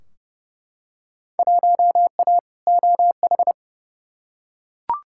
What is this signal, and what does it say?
Morse code, heard as a keyed tone, about 30 wpm, 700 Hz. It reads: 1A OH
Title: Field Day Exchanges - 30wpm